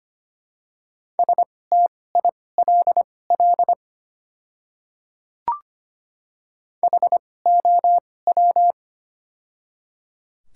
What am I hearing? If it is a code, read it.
Morse code, heard as a keyed tone, about 25 wpm, spaced as for 18 wpm, 700 Hz. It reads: STILL HOW
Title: 2x Character Spacing - Top 500 Words - 25wpm